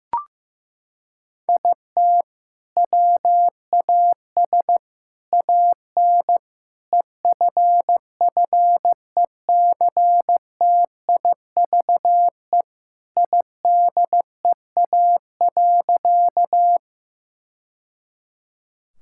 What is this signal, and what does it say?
Morse code, heard as a keyed tone, about 15 wpm, 700 Hz. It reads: IT WAS AN EFFECTIVE IDEA.
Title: Sentences from Top 1000 Words - Review Part 6 - 15wpm